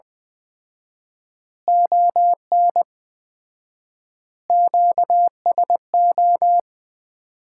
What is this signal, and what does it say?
Morse code, heard as a keyed tone, about 20 wpm, 700 Hz. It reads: ON QSO
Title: Common QSO Phrases - 4x Word-Spacing - 20wpm